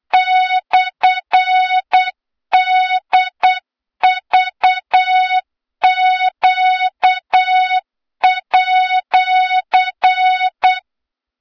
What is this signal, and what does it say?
Morse code, heard as a keyed tone, about 8 wpm, 3700 Hz, with noise. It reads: /DVQ@
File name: Morze-HA2MN / hang / 23 GYAKORLAT